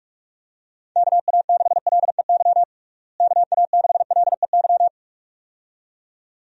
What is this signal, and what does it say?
Morse code, heard as a keyed tone, about 42 wpm, 700 Hz. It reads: KA6LEY KA6LEY
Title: USA Call Signs - 6 Characters - Repeated 2x - 45wpm